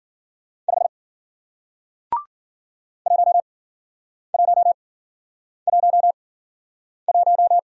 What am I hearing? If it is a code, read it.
Morse code, heard as a keyed tone, about 55 wpm, spaced as for 9 wpm, 700 Hz. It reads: 5 1111